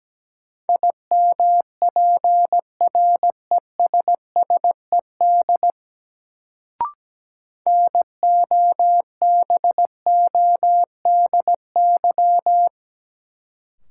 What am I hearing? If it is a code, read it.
Morse code, heard as a keyed tone, about 17 wpm, 700 Hz. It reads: IMPRESSED NOBODY
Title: Sentences from Top 1600 Words - New Word Drill Alt - 17wpm